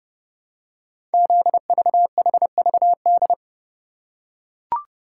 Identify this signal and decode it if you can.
Morse code, heard as a keyed tone, about 28 wpm, 700 Hz. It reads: ZVHVD
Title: Code Groups - 5 Letters - 30wpm